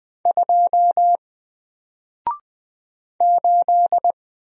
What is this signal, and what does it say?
Morse code, heard as a keyed tone, about 20 wpm, 700 Hz. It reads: 2 8
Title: Single Character (Letter:Number:Punctuation:Prosign) 20wpm